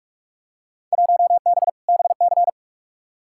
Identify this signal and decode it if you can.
Morse code, heard as a keyed tone, about 42 wpm, 700 Hz. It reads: 1B BC
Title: Field Day Exchanges - 45wpm